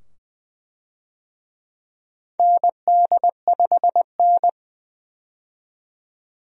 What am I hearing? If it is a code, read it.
Morse code, heard as a keyed tone, about 21 wpm, 700 Hz. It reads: ND5N
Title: USA Call Signs - 4 Characters - 20wpm